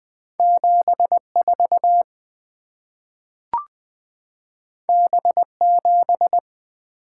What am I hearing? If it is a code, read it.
Morse code, heard as a keyed tone, about 20 wpm, 700 Hz. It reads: 74 B7
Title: Two Letter-Number Combinations - 20wpm